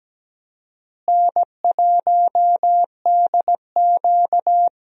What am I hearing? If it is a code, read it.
Morse code, heard as a keyed tone, about 17 wpm, 700 Hz. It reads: N1DQ
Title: USA Call Signs - 4 Characters - Repeated 2x - 17wpm